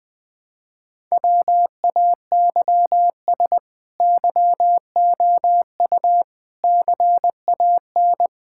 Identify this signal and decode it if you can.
Morse code, heard as a keyed tone, about 20 wpm, 700 Hz. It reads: WAYS YOU CAN
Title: Sets of 8 Words - 20wpm